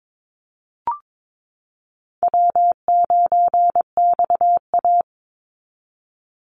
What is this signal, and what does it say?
Morse code, heard as a keyed tone, about 22 wpm, 700 Hz. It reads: W9XA
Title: Call Signs Repeated 2x - 22wpm